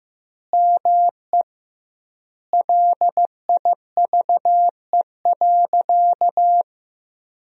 Morse code, heard as a keyed tone, about 15 wpm, 700 Hz. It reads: ME LIVE.
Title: 2x Word-Spacing - Sentences from Top 300 Words - 15wpm